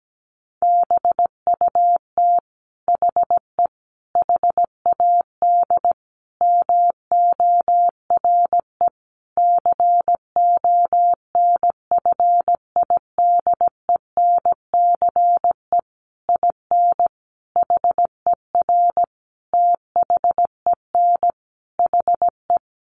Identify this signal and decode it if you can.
Morse code, heard as a keyed tone, about 17 wpm, 700 Hz. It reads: BUT HE HAD MORE CONFIDENCE IN HER THEN HE
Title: Sentences from Top 1500 Words 17wpm